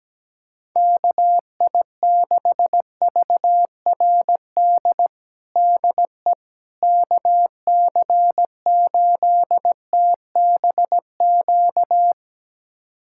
Morse code, heard as a keyed tone, about 17 wpm, 700 Hz. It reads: KI6VRD DE KC8TBQ
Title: USA Callsign DE Callsign 17wpm